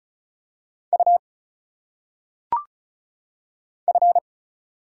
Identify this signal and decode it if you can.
Morse code, heard as a keyed tone, about 36 wpm, 700 Hz. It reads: U F
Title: Single Letters - Phonetic Pronunciation 35wpm